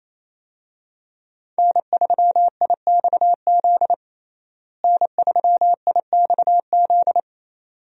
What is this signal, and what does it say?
Morse code, heard as a keyed tone, about 28 wpm, 700 Hz. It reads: N3IXZ N3IXZ
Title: USA Call Signs - 5 Characters - Repeated 2x - 28wpm